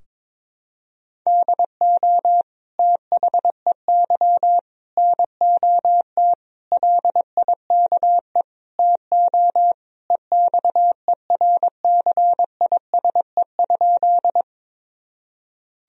Morse code, heard as a keyed tone, about 22 wpm, 700 Hz. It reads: DO THEY NOT LIKE TO EXERCISE?
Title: Sentences from Top 1000 Words - Review Part 4 - 22wpm